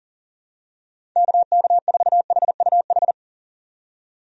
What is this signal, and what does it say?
Morse code, heard as a keyed tone, about 40 wpm, 700 Hz. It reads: KK4HUH
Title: USA Call Signs - 6 Characters - 40wpm